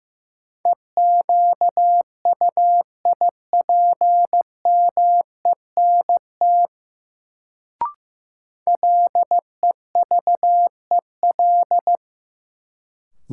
Morse code, heard as a keyed tone, about 15 wpm, 700 Hz. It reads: EQUIPMENT LEVEL
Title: Sentences from Top 700 Words - New Word Drill - 15wpm